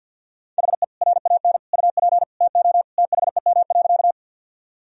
Morse code, heard as a keyed tone, about 50 wpm, 700 Hz. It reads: HE RAN UP TO THEM.
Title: Sentences from Top 1000 Words - Review Part 3 - 50wpm